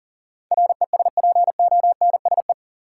Morse code, heard as a keyed tone, about 39 wpm, 700 Hz. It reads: RESPONSE
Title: Sentences from Top 1400 Words - New Word Drill Alt - 40wpm